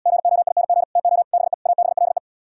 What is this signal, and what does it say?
Morse code, heard as a keyed tone, about 79 wpm, 700 Hz. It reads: GOING TO BE ABLE